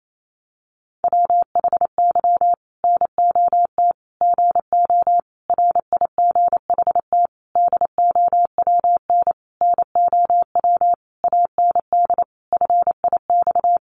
Morse code, heard as a keyed tone, about 28 wpm, 700 Hz. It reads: WHY NOT GO RIGHT DOWN NOW AND FIX
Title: Sentences from Top 1400 Words 28wpm